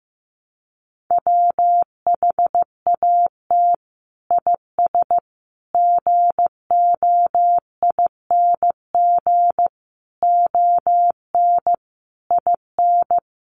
Morse code, heard as a keyed tone, about 15 wpm, 700 Hz. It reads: WHAT IS GOING ON IN
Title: Sets of 5 Words Encore2 15wpm